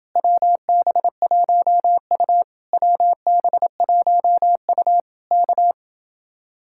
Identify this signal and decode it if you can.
Morse code, heard as a keyed tone, about 29 wpm, 700 Hz. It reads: WB1U WB1U K